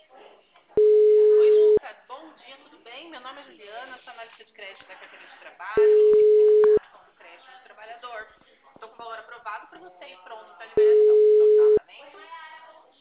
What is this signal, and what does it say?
Ringback tone (Europe, 425 Hz), with speech.